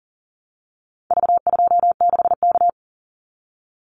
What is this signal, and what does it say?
Morse code, heard as a keyed tone, about 40 wpm, 700 Hz. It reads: V26K
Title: Call Signs Repeated 2x - 40wpm